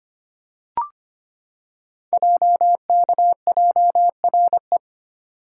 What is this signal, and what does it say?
Morse code, heard as a keyed tone, about 25 wpm, 700 Hz. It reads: JKJRE
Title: Code Groups - 5 Letters - 25wpm